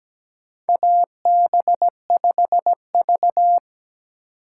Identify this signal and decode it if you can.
Morse code, heard as a keyed tone, about 17 wpm, 700 Hz. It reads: AB5V
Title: USA Call Signs - 4 Characters - 3x Character Spacing - 17wpm